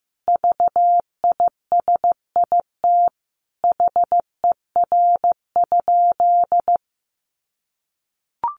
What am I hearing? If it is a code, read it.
Morse code, heard as a keyed tone, about 15 wpm, 700 Hz. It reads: VISIT HER?
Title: Sentences from Top 900 Words 15wpm